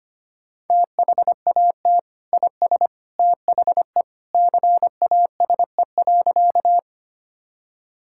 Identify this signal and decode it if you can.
Morse code, heard as a keyed tone, about 25 wpm, 700 Hz. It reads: THAT IS THE CASE.
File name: practice / 25wpm / Sentences with 2-Word Phrases - 25wpm